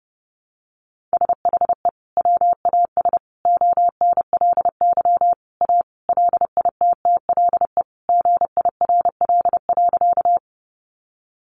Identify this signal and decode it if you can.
Morse code, heard as a keyed tone, about 30 wpm, 700 Hz. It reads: SHE WAS ONLY A LITTLE GIRL.
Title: Sentences from Top 400 Words 30wpm